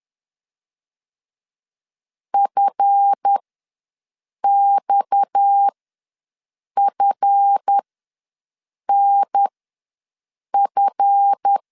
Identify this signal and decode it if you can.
Morse code, heard as a keyed tone, about 10 wpm, spaced as for 6 wpm, 800 Hz. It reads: FXFNF
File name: LEZIONI / 7 GF9Z / 7 GF ws 10wpm 3918